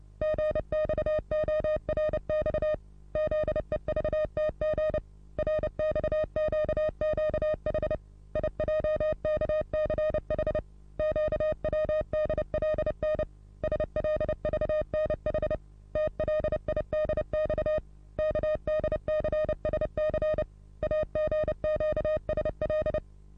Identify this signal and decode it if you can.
Morse code, heard as a keyed tone, about 29 wpm, 620 Hz. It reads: GXORX ZEVTG RXQQH IJKCH QWDLN SLVNH TLIDX KDCSC AGQSL
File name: repos / Morsekurs / DJ4UF Morsen27